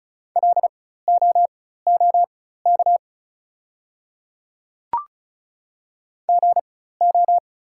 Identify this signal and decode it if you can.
Morse code, heard as a keyed tone, about 36 wpm, spaced as for 16 wpm, 690 Hz. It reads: LOOK GO
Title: 4x Character Spacing - Top 200 Words - 35wpm